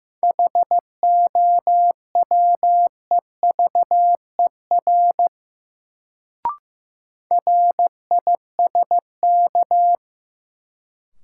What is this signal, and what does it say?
Morse code, heard as a keyed tone, about 15 wpm, 700 Hz. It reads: HOWEVER RISK
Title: Sentences from Top 600 Words - New Word Drill - 15wpm